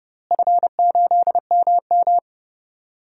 Morse code, heard as a keyed tone, about 30 wpm, 700 Hz. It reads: F8MM
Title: DXCC-100 Call Signs - 4 Characters - Repeated 2x - 30wpm